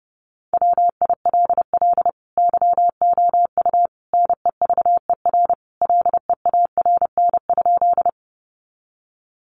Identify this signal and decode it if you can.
Morse code, heard as a keyed tone, about 28 wpm, 700 Hz. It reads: WILL YOU NEVER LEARN?
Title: Sentences from Top 300 Words 30wpm